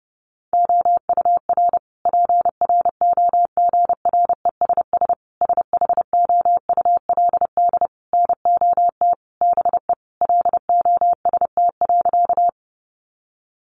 Morse code, heard as a keyed tone, about 30 wpm, 700 Hz. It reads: OUR PROGRESS SHOULD NOT BE LOST.
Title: Sentences from Top 1400 Words 30wpm